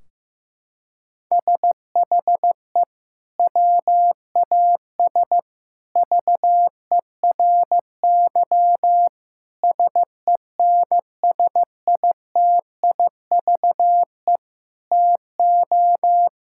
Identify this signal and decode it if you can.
Morse code, heard as a keyed tone, about 16 wpm, 700 Hz. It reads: SHE WAS VERY SENSITIVE TO
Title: Sentences from Top 1600 Words 15wpm